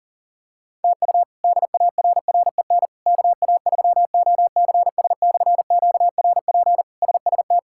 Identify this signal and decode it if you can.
Morse code, heard as a keyed tone, about 40 wpm, 700 Hz. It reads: TU DARREN KA3OCS/QRP SST
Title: K1USN SST QSO Practice - 40wpm